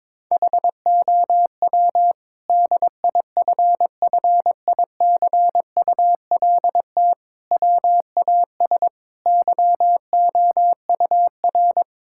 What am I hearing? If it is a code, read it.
Morse code, heard as a keyed tone, about 22 wpm, 700 Hz. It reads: HOW DIFFICULT WAS YOUR
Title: Sentences from Top 1000 Words - Review Part 6 - 22wpm